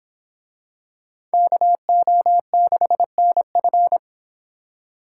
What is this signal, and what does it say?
Morse code, heard as a keyed tone, about 26 wpm, 700 Hz. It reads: KO6NF